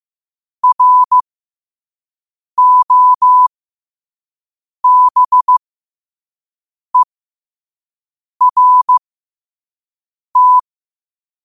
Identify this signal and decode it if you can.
Morse code, heard as a keyed tone, about 15 wpm, spaced as for 5 wpm, 1000 Hz. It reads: ROBERT
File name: qotdfiles / 5wpm